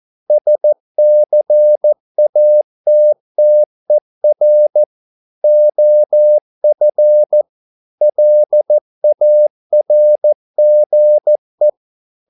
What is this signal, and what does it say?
Morse code, heard as a keyed tone, about 14 wpm, 600 Hz. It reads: SCATTER OF LARGE